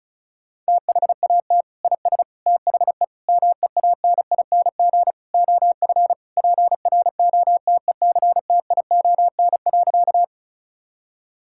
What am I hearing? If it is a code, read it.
Morse code, heard as a keyed tone, about 36 wpm, 700 Hz. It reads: THAT IS THE MEANING OF PROTECTION.
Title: Sentences from Top 1600 Words 35wpm